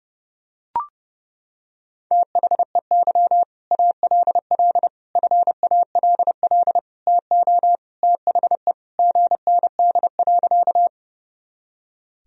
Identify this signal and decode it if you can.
Morse code, heard as a keyed tone, about 32 wpm, 700 Hz. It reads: THEY ALL FALL TO THE GND.
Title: Sentences from Top 500 Words and Abbreviations 30wpm